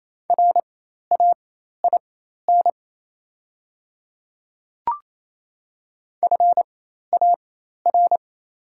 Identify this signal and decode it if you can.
Morse code, heard as a keyed tone, about 27 wpm, spaced as for 13 wpm, 700 Hz. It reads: RAIN FAR
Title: 4x Character Spacing - Top 300 Words - 28wpm